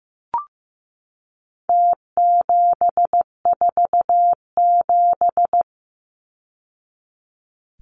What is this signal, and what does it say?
Morse code, heard as a keyed tone, about 15 wpm, 700 Hz. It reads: T747
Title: Numbers - Contest Style - 4-Digit 15wpm